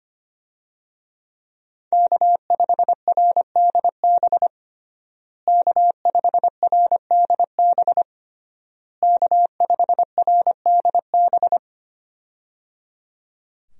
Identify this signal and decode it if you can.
Morse code, heard as a keyed tone, about 25 wpm, 700 Hz. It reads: K5RDB K5RDB K5RDB
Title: USA Call Signs - 5 Characters - Repeated 3x - 25wpm